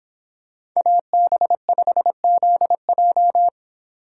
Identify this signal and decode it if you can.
Morse code, heard as a keyed tone, about 24 wpm, 700 Hz. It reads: AB5ZJ